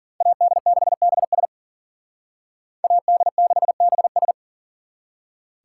Morse code, heard as a keyed tone, about 50 wpm, 700 Hz. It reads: AD6BS AD6BS